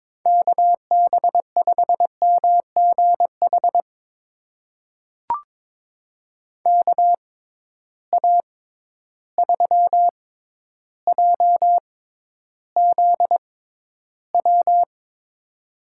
Morse code, heard as a keyed tone, about 22 wpm, spaced as for 11 wpm, 700 Hz. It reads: KB5MGH KA3JZW
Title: USA Call Signs - 6 Characters - 6x Character Spacing - 22wpm